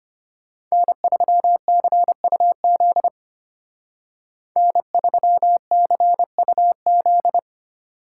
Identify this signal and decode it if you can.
Morse code, heard as a keyed tone, about 33 wpm, 700 Hz. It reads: N3CUZ N3CUZ